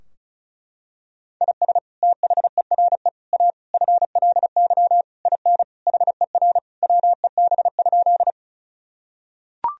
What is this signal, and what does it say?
Morse code, heard as a keyed tone, about 36 wpm, 700 Hz. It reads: IS THERE A FLY IN HER WEB?
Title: Sentences from Top 1000 Words - Review Part 4 - 35wpm